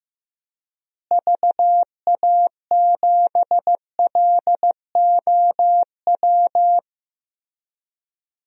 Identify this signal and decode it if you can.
Morse code, heard as a keyed tone, about 15 wpm, 700 Hz. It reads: VA7LOW
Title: DXCC-100 Call Signs - 6 Characters - Repeated 3x - 15wpm